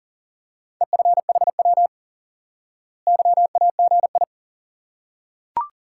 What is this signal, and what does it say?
Morse code, heard as a keyed tone, about 40 wpm, 700 Hz. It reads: EFHW YAGI
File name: IQE Course 3 - Lesson 89 - YAGI - Two Elements - 40wpm